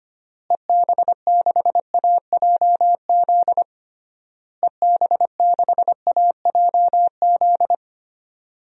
Morse code, heard as a keyed tone, about 25 wpm, 700 Hz. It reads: EB6AJZ EB6AJZ
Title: DXCC-100 Call Signs - 6 Characters - Repeated 3x - 25wpm